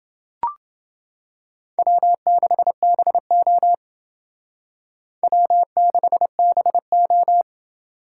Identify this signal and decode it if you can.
Morse code, heard as a keyed tone, about 30 wpm, 700 Hz. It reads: W6BO W6BO